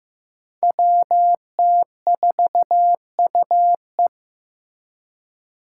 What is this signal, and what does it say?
Morse code, heard as a keyed tone, about 15 wpm, 700 Hz. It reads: WT4UE
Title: Five Challenging Letter-Number Combinations 15wpm